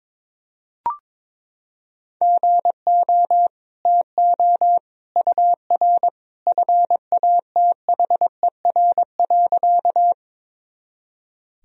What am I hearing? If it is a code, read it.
Morse code, heard as a keyed tone, about 22 wpm, 700 Hz. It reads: GO TO UR FATHER.
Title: Sentences from Top 300 Words and Abbreviations 22wpm